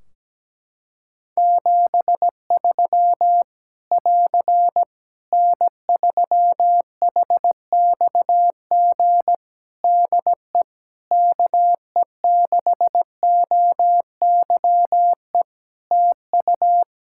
Morse code, heard as a keyed tone, about 17 wpm, 700 Hz. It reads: 73 <AR> N3HXG DE KE6OYE TU
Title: Practice QSO - Exchange 4 - Ending - 17wpm